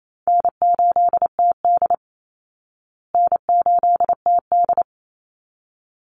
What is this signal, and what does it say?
Morse code, heard as a keyed tone, about 28 wpm, 700 Hz. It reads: N8TD N8TD
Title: Easy Call Signs Repeated 2x - 28wpm